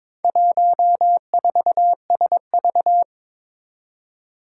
Morse code, heard as a keyed tone, about 23 wpm, 700 Hz. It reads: 14SV
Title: Four Challenging Letter-Number Combinations 22wpm